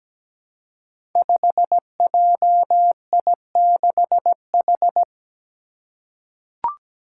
Morse code, heard as a keyed tone, about 17 wpm, 700 Hz. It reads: 5JI6H
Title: Five Challenging Letter-Number Combinations 17wpm